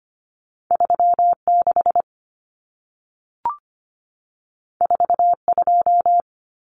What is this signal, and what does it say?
Morse code, heard as a keyed tone, about 25 wpm, 700 Hz. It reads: 36 42
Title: Common Cut Numbers - 2-Digit 25wpm